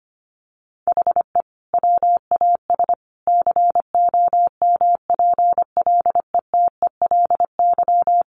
Morse code, heard as a keyed tone, about 25 wpm, 700 Hz. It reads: HE WAS COMPLETELY
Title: Sentences from Top 1100 Words 25wpm